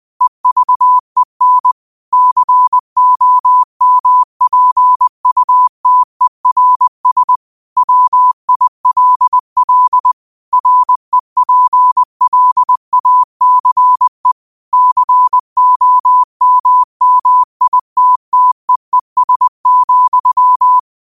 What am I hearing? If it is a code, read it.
Morse code, heard as a keyed tone, about 20 wpm, 1000 Hz. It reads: EVEN COMPUTERS WILL REPLACE COMMITTEES,
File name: qotdfiles / 20wpm